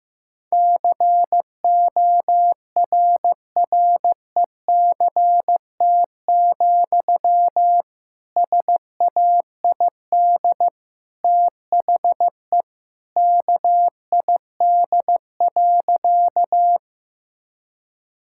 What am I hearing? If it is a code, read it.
Morse code, heard as a keyed tone, about 16 wpm, 700 Hz. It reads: CORRECT, SAID THE KID.
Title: Sentences from Top 1600 Words 15wpm